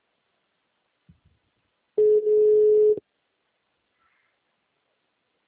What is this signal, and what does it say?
Ringback tone (Europe, 425 Hz).
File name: out- -2011- - - .273